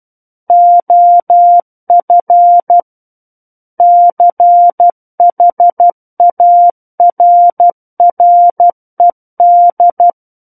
Morse code, heard as a keyed tone, about 12 wpm, 700 Hz. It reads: OF CHARRED
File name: Book One / 03-Chapter 3 - 12 WPM